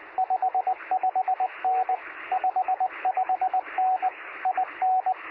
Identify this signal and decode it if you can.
Morse code, heard as a keyed tone, about 20 wpm, 750 Hz, with radio noise. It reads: 55N 55N IN
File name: sm7lcb / 1819z sp2wpy 23cm